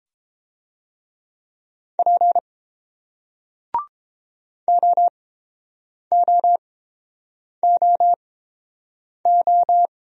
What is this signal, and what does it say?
Morse code, heard as a keyed tone, about 31 wpm, spaced as for 7 wpm, 700 Hz. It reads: P OOOO